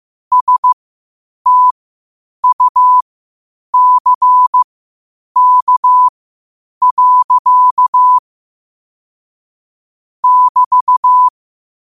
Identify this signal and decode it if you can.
Morse code, heard as a keyed tone, about 15 wpm, spaced as for 8 wpm, 1000 Hz. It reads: STUCK. <BT>
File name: qotdfiles / 8wpm